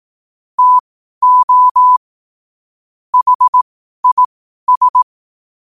Morse code, heard as a keyed tone, about 18 wpm, spaced as for 12 wpm, 1000 Hz. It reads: TO HIS
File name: qotdfiles / 12wpm